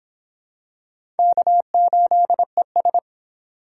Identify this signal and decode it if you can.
Morse code, heard as a keyed tone, about 25 wpm, 700 Hz. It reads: K8ES